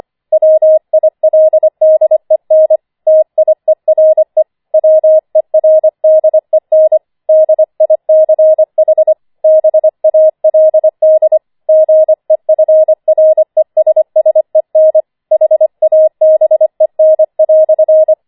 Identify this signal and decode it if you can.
Morse code, heard as a keyed tone, about 25 wpm, 630 Hz. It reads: WILDEN TIERE WERDEN DICH BALD GEFRESSEN HABEN"
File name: 25wpm